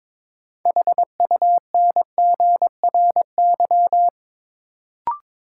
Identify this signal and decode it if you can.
Morse code, heard as a keyed tone, about 22 wpm, 700 Hz. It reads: HUNGRY
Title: Sentences from Top 1400 Words - New Word Drill - 22wpm